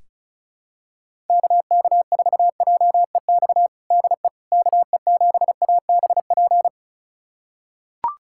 Morse code, heard as a keyed tone, about 34 wpm, 700 Hz. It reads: KK4JEX DE KE7ABP
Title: USA Callsign DE Callsign - 6 Characters - 35wpm